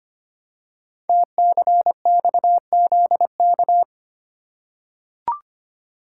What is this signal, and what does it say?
Morse code, heard as a keyed tone, about 25 wpm, 700 Hz. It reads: TCXZK
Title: Code Groups - 5 Letters - 25wpm